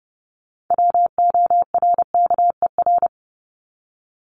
Morse code, heard as a keyed tone, about 30 wpm, 700 Hz. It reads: WORKER
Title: ES:S Suffix - Top 100 Words with Base Words 30wpm